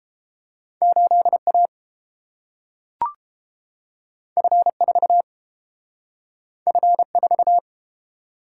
Morse code, heard as a keyed tone, about 30 wpm, 700 Hz. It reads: 8A F4 F4